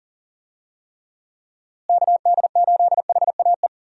Morse code, heard as a keyed tone, about 39 wpm, 700 Hz. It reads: KD8HAE
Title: USA Call Signs - 6 Characters - Repeated 3x - 40wpm